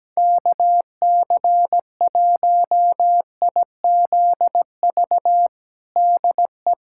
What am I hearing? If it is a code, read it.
Morse code, heard as a keyed tone, about 17 wpm, 700 Hz. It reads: KC1IZV DE
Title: USA Callsign DE Callsign 17wpm